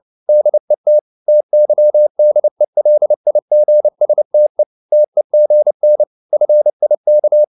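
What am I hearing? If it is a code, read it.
Morse code, heard as a keyed tone, about 29 wpm, 600 Hz. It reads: DET TYDELIGSTE TEGN FIK
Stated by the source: Fast (QRQ) Morse at 29 wpm, from CW med Gnister.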